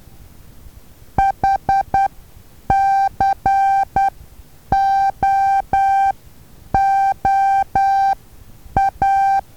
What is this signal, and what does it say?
Morse code, heard as a keyed tone, about 10 wpm, spaced as for 8 wpm, 780 Hz. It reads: HCOOA
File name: Taviro HA6VA A